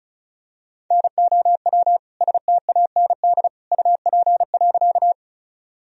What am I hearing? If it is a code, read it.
Morse code, heard as a keyed tone, about 36 wpm, 700 Hz. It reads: NOW STAND UP.
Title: Sentences from Top 1000 Words - Review Part 1 - 35wpm